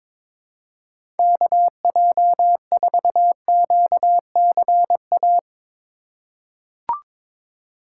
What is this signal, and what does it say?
Morse code, heard as a keyed tone, about 21 wpm, 700 Hz. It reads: KJ4QCA
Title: USA Call Signs - 4x Character Spacing - 22wpm